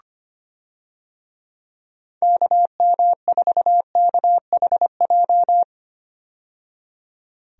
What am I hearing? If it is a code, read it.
Morse code, heard as a keyed tone, about 25 wpm, 700 Hz. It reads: KM4KHJ
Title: USA Call Signs - 6 Characters - 25wpm